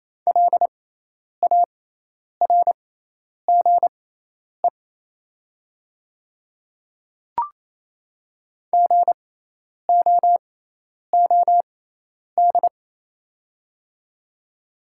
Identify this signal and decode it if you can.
Morse code, heard as a keyed tone, about 28 wpm, spaced as for 10 wpm, 700 Hz. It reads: LARGE GOOD
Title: 6x Character Spacing - Top 200 Words - 28wpm